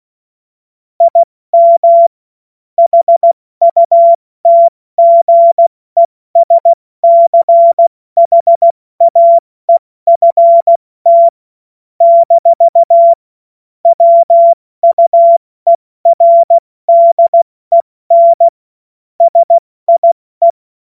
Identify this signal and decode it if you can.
Morse code, heard as a keyed tone, about 16 wpm, 680 Hz. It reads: IM HUTGESCHAEFT - WUERDEN SIE
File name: Morsekurs / Witze / Witz 06 Februar 2024 70 F80